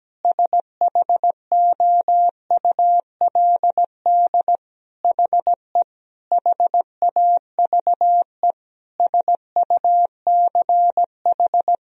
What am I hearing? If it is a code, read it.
Morse code, heard as a keyed tone, about 17 wpm, 700 Hz. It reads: SHOULD HE HAVE SUCH
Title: Sentences from Top 1000 Words - Review Part 2 - 17wpm